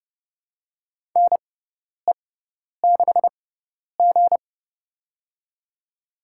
Morse code, heard as a keyed tone, about 29 wpm, spaced as for 10 wpm, 700 Hz. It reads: NE6G
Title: USA Call Signs - 4 Characters - 6x Character Spacing - 30wpm